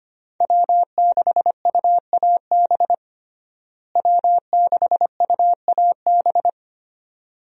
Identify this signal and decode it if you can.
Morse code, heard as a keyed tone, about 25 wpm, 700 Hz. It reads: W6UAB W6UAB
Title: USA Call Signs - 5 Characters - Repeated 2x - 25wpm